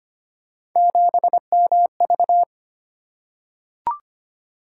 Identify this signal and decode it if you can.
Morse code, heard as a keyed tone, about 25 wpm, 700 Hz. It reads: 7MV
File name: Three Letter-Number Combinations - 25wpm